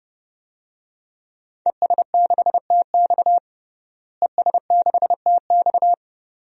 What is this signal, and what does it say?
Morse code, heard as a keyed tone, about 32 wpm, 700 Hz. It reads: ES6TX ES6TX
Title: DXCC-100 Call Signs - 5 Characters - Repeated 2x - 30wpm